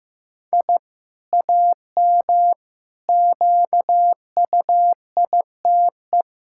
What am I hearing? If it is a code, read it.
Morse code, heard as a keyed tone, about 16 wpm, 700 Hz. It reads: I AM QUITE
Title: Sentences from Top 1000 Words - Review Part 5 - 15wpm